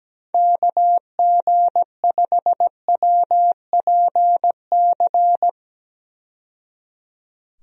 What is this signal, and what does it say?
Morse code, heard as a keyed tone, about 17 wpm, 700 Hz. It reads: KG5WPC
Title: USA Call Signs - 6 Characters - 17wpm